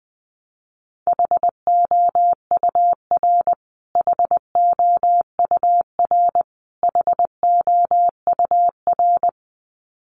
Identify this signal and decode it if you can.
Morse code, heard as a keyed tone, about 20 wpm, 700 Hz. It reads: HOUR HOUR HOUR
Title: Top 500 Words Repeated 20wpm